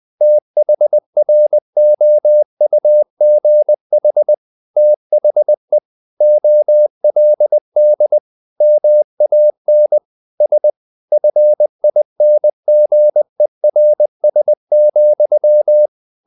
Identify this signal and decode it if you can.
Morse code, heard as a keyed tone, about 20 wpm, 600 Hz. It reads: THROUGH THE OLD MAN S FINGERS,
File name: the old man and the sea / the old man and the sea part 1 20wpm